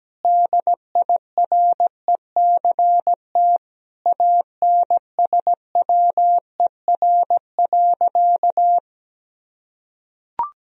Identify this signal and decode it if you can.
Morse code, heard as a keyed tone, about 17 wpm, 700 Hz. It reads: DIRECT ANSWER.
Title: Sentences from Top 1000 Words - Review Part 2 - 17wpm